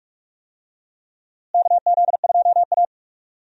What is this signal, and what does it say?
Morse code, heard as a keyed tone, about 42 wpm, 700 Hz. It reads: KZ2A
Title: USA Call Signs - 4 Characters - Repeated 3x - 45wpm